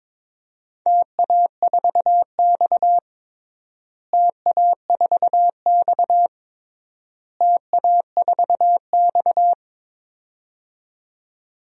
Morse code, heard as a keyed tone, about 22 wpm, 700 Hz. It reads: TA4X TA4X TA4X
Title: DXCC-100 Call Signs - 4 Characters - Repeated 3x - 22wpm